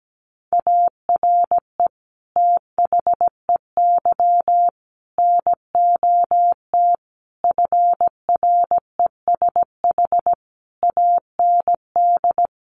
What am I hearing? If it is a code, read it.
Morse code, heard as a keyed tone, about 17 wpm, 700 Hz. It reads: ARE THEY NOT FRESH AND
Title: Sentences from Top 1200 Words 17wpm